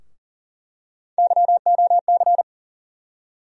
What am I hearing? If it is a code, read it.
Morse code, heard as a keyed tone, about 40 wpm, 700 Hz. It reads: YOC
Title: USA Call Sign Postfixes - 3 Letters 40wpm